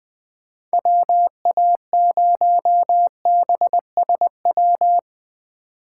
Morse code, heard as a keyed tone, about 20 wpm, 700 Hz. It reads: WA0BSW
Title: USA Call Signs - 6 Characters - 4x Character Spacing - 20wpm